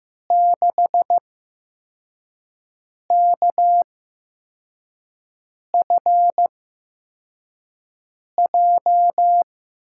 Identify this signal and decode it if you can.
Morse code, heard as a keyed tone, about 15 wpm, spaced as for 4 wpm, 700 Hz. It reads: 6KFJ
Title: USA Call Signs - 5 Characters - 8x Character Spacing - 15wpm